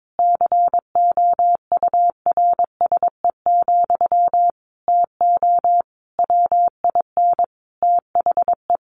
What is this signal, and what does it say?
Morse code, heard as a keyed tone, about 22 wpm, 700 Hz. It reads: COURSE, TO WIN THE
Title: Sentences from Top 1500 Words 22wpm